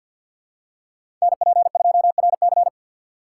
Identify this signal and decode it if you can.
Morse code, heard as a keyed tone, about 53 wpm, 700 Hz. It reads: NP2RC
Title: USA Call Signs - 4x Character Spacing - 50wpm